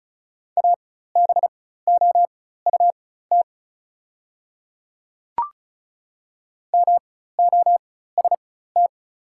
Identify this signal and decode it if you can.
Morse code, heard as a keyed tone, about 36 wpm, spaced as for 16 wpm, 690 Hz. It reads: ABOUT MOST
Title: 4x Character Spacing - Top 100 Words - 35wpm